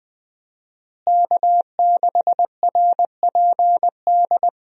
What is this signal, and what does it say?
Morse code, heard as a keyed tone, about 20 wpm, 700 Hz. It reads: K6RPD
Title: USA Call Signs - 5 Characters - Repeated 2x - 20wpm